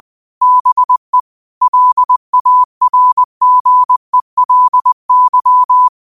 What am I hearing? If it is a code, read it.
Morse code, heard as a keyed tone, about 20 wpm, 1000 Hz. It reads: BE LARGELY
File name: 20wpm